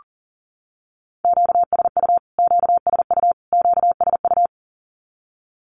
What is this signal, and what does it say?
Morse code, heard as a keyed tone, about 40 wpm, 700 Hz. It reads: QSU QSU QSU
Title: Q-Codes Repeated 3x 40wpm